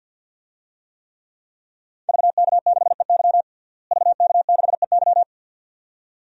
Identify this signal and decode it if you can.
Morse code, heard as a keyed tone, about 52 wpm, 700 Hz. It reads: VK6EY VK6EY
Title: DXCC-100 Call Signs - 5 Characters - Repeated 2x - 50wpm